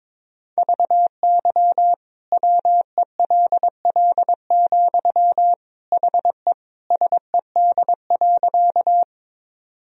Morse code, heard as a keyed tone, about 22 wpm, 700 Hz. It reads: VY WELL, HE SED.
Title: Sentences from Top 200 Words and Abbreviations 22wpm